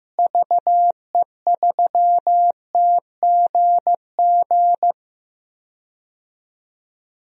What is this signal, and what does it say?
Morse code, heard as a keyed tone, about 15 wpm, 700 Hz. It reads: VE3TGG
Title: DXCC-100 Call Signs - 6 Characters - Repeated 3x - 15wpm